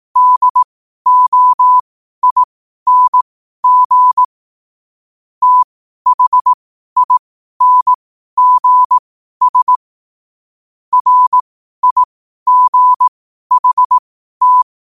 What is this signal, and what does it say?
Morse code, heard as a keyed tone, about 18 wpm, spaced as for 12 wpm, 1000 Hz. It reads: DOING THINGS RIGHT
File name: qotdfiles / 12wpm